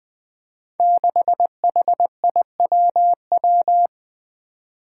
Morse code, heard as a keyed tone, about 20 wpm, 700 Hz. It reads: 6HIWW
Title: Five Challenging Letter-Number Combinations 20wpm